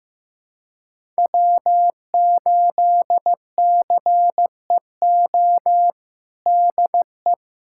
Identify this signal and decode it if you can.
Morse code, heard as a keyed tone, about 16 wpm, 700 Hz. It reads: W8CEO DE
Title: USA Callsign DE Callsign 15wpm